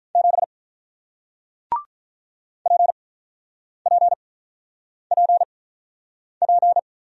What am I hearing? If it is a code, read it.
Morse code, heard as a keyed tone, about 48 wpm, spaced as for 9 wpm, 690 Hz. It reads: 7 PPPP